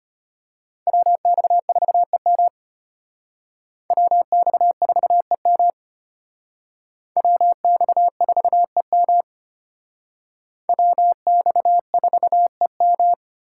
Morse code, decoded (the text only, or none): WX4EM WX4EM WX4EM WX4EM